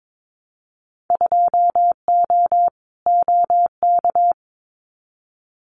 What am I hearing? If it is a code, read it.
Morse code, heard as a keyed tone, about 22 wpm, 700 Hz. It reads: 2O OK
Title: Winter Field Day Exchanges Encore 22wpm